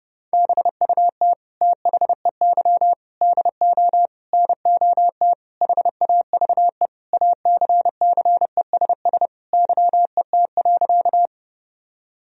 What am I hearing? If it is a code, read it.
Morse code, heard as a keyed tone, about 29 wpm, 700 Hz. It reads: BUT THEY DO NOT HAVE ACCESS YET.
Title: Sentences from Top 1000 Words - Review Part 4 - 30wpm